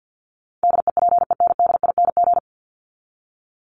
Morse code, heard as a keyed tone, about 50 wpm, 700 Hz. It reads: DEPENDING
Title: ING Suffix - Top 100 Words with Base Words 50wpm